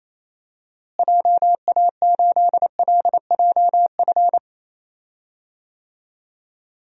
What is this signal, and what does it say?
Morse code, heard as a keyed tone, about 28 wpm, 700 Hz. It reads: JA8LJF
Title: DXCC-100 Call Signs - 6 Characters - Repeated 3x - 28wpm